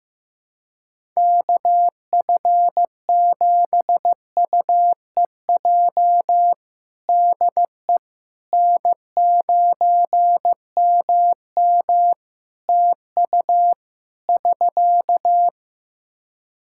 Morse code, heard as a keyed tone, about 15 wpm, 700 Hz. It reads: KF7UEJ DE N9MM TU <SK>
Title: Practice QSO - Exchange 4 - Ending - 15wpm